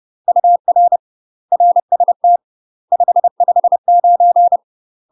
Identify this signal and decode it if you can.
Morse code, heard as a keyed tone, about 30 wpm, 700 Hz. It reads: UR RST 559
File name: UK MASTER QSOs TRAINING / 30WPM / UK MASTER QSOs 0044